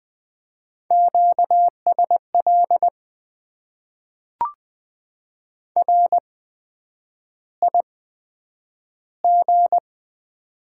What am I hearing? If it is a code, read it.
Morse code, heard as a keyed tone, about 20 wpm, spaced as for 9 wpm, 700 Hz. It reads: QSL RIG
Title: Top 90 QSO Elements - 8x Character Spacing - 20wpm